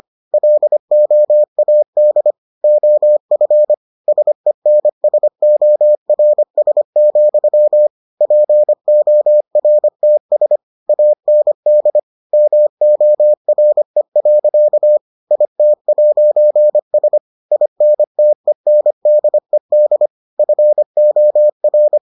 Morse code, heard as a keyed tone, about 25 wpm, 600 Hz. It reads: LOAD OF SENSORS, PORTS AND MORE. IT'S INTENDED FOR